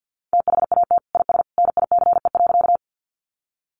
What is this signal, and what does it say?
Morse code, heard as a keyed tone, about 50 wpm, 700 Hz. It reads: THAT IS NICE.
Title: Sentences from Top 1100 Words 50wpm